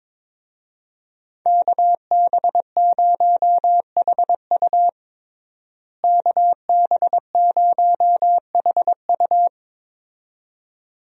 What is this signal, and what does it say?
Morse code, heard as a keyed tone, about 21 wpm, 700 Hz. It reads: KB0HU KB0HU
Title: USA Call Signs - 5 Characters - Repeated 2x - 22wpm